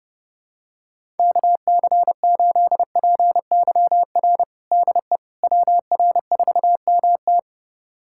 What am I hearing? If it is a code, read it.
Morse code, heard as a keyed tone, about 30 wpm, 700 Hz. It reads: KC8PYR DE WR4MT
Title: USA Callsign DE Callsign 30wpm